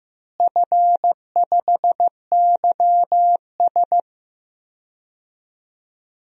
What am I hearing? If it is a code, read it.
Morse code, heard as a keyed tone, about 15 wpm, 700 Hz. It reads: F5YS
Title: DXCC-100 Call Signs - 4 Characters - Repeated 2x - 15wpm